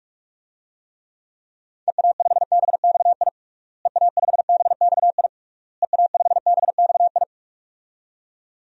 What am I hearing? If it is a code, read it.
Morse code, heard as a keyed tone, about 41 wpm, 700 Hz. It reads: EA5BXI EA5BXI EA5BXI
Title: DXCC-100 Call Signs - 6 Characters - Repeated 3x - 45wpm